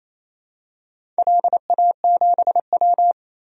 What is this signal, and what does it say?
Morse code, heard as a keyed tone, about 28 wpm, 700 Hz. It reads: LA7W
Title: DXCC-100 Call Signs - 4 Characters - 28wpm